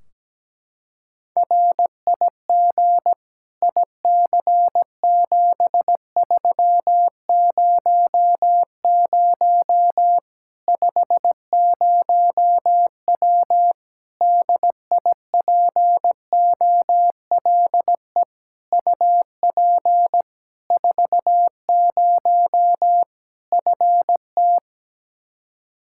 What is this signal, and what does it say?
Morse code, heard as a keyed tone, about 17 wpm, 700 Hz. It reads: RIG IC7300 50W DIPOLE UP 40 FT
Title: Practice QSO - Exchange 2 - Info - 17wpm